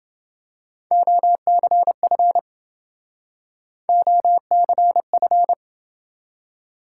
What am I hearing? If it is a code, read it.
Morse code, heard as a keyed tone, about 29 wpm, 700 Hz. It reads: OCF OCF